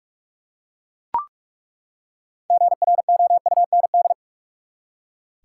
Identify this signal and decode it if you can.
Morse code, heard as a keyed tone, about 43 wpm, 700 Hz. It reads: GROUND
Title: Six Letter Words - 45wpm